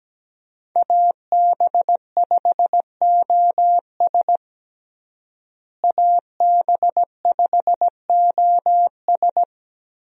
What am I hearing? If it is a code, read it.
Morse code, heard as a keyed tone, about 17 wpm, 700 Hz. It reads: AB5OS AB5OS
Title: USA Call Signs - 5 Characters - Repeated 2x - 17wpm